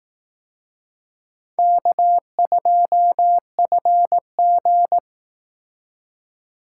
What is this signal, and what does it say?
Morse code, heard as a keyed tone, about 18 wpm, 700 Hz. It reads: K2FG
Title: Speed Racing - USA 1x2 Call Signs 15wpm